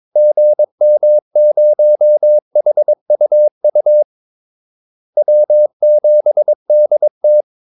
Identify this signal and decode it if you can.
Morse code, heard as a keyed tone, about 22 wpm, 600 Hz. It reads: GM0HUU W7DT
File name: Callsigns-22wpm